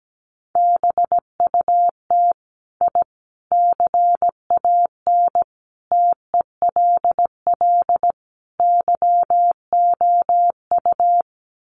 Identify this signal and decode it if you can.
Morse code, heard as a keyed tone, about 17 wpm, 700 Hz. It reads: BUT I CAN TELL YOU
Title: Sets of 5 Words Encore 17wpm